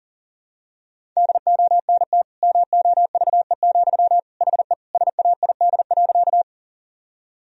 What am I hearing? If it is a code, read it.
Morse code, heard as a keyed tone, about 40 wpm, 700 Hz. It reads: DONT MOVE, HE SAID.
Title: Sentences from Top 1000 Words - Review Part 1 - 40wpm